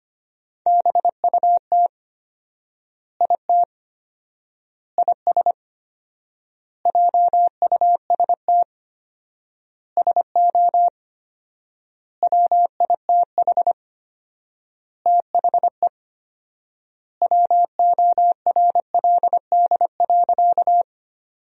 Morse code, heard as a keyed tone, about 25 wpm, 700 Hz. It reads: BUT IT IS JUST SO WITH THE WORLD.
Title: 4x Word-Spacing - Sentences from Top 200 Words - 25wpm